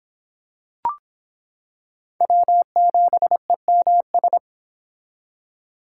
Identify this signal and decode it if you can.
Morse code, heard as a keyed tone, about 26 wpm, 700 Hz. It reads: W7EMS